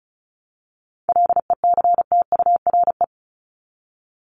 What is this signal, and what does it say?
Morse code, heard as a keyed tone, about 34 wpm, 700 Hz. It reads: LECTURE
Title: Top 1501-1750 Words 35wpm